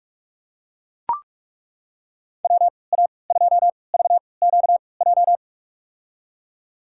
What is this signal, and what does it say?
Morse code, heard as a keyed tone, about 42 wpm, spaced as for 25 wpm, 690 Hz. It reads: WA2VQJ
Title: USA Call Signs - 6 Characters - 3x Character Spacing - 45wpm